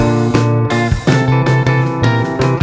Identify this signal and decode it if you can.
There is no code.